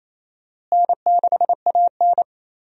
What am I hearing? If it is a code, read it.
Morse code, heard as a keyed tone, about 27 wpm, 700 Hz. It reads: N6AN
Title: USA Call Signs - 4 Characters - Repeated 2x - 28wpm